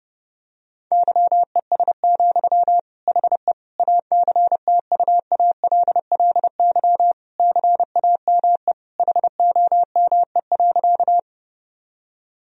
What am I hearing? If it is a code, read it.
Morse code, heard as a keyed tone, about 32 wpm, 700 Hz. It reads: YES, HE ACTUALLY CAME HOME.
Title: Sentences from Top 1000 Words - Review Part 4 - 30wpm